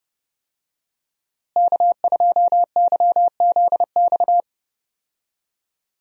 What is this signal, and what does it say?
Morse code, heard as a keyed tone, about 32 wpm, 700 Hz. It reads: K2YZX
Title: USA Call Signs - 5 Characters - Repeated 3x - 30wpm